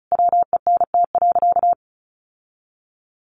Morse code, heard as a keyed tone, about 34 wpm, 700 Hz. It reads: WENT.
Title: Sentences from Top 1300 Words 35wpm